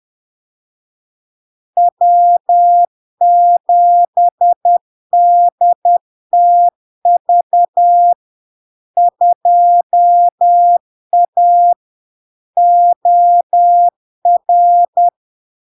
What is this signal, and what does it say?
Morse code, heard as a keyed tone, about 10 wpm, 700 Hz. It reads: W7DTV 2A OR
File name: LARK-FD-2022-Log-in-CW-10wpm-700-Hz